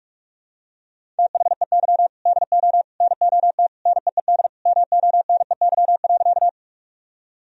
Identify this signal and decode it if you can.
Morse code, heard as a keyed tone, about 46 wpm, 700 Hz. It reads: THEY DO NOT NEED MONEY.
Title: Sentences from Top 1000 Words - Review Part 1 - 45wpm